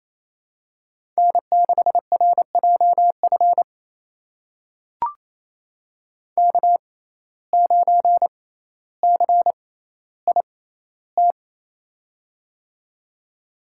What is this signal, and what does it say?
Morse code, heard as a keyed tone, about 28 wpm, spaced as for 14 wpm, 700 Hz. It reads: N6RJF K9CIT
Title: USA Call Signs - 5 Characters - 6x Character Spacing - 28wpm